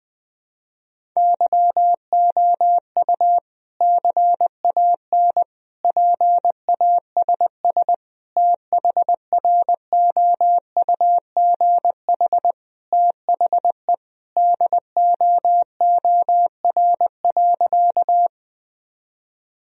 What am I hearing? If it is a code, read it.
Morse code, heard as a keyed tone, about 20 wpm, 700 Hz. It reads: YOU CAN PASS THROUGH THE DOOR.
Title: Sentences from Top 1000 Words - Review Part 1 - 20wpm